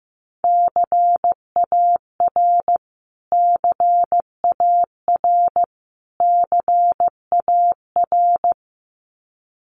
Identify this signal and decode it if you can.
Morse code, heard as a keyed tone, about 15 wpm, 700 Hz. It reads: CAR CAR CAR
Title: Top 500 Words Repeated 15wpm